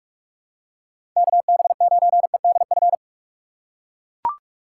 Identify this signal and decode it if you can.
Morse code, heard as a keyed tone, about 42 wpm, 700 Hz. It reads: KB9EDF
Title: USA Call Signs - 4x Character Spacing - 45wpm